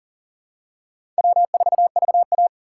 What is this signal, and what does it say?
Morse code, heard as a keyed tone, about 41 wpm, 700 Hz. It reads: W4VA
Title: USA Call Signs - 4 Characters - 2x Character Spacing - 40wpm